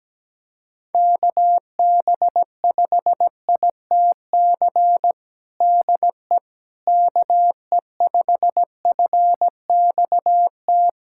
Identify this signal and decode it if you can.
Morse code, heard as a keyed tone, about 17 wpm, 700 Hz. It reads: KB5ITC DE KE5FXT
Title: USA Callsign DE Callsign - 6 Characters - 17wpm